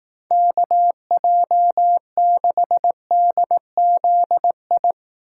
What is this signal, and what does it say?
Morse code, heard as a keyed tone, about 18 wpm, 700 Hz. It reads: KJ6DZI
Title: Speed Racing - USA Call Signs 15wpm